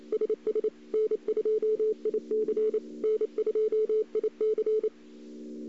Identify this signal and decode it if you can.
Morse code, heard as a keyed tone, about 28 wpm, 430 Hz. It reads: SS N2IC N2IC
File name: CW / 2008-ss-cw-k5tr-op-n5rz -02:00